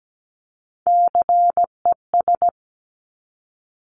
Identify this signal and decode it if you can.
Morse code, heard as a keyed tone, about 17 wpm, 700 Hz. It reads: CES
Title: Top Three Letter Combinations in Words 17wpm